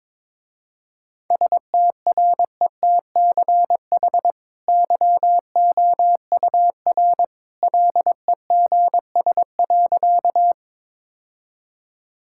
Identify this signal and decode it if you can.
Morse code, heard as a keyed tone, about 22 wpm, 700 Hz. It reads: STRETCH YOUR LEGS.
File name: practice / 22wpm / Easy Sentences with Hints - 22wpm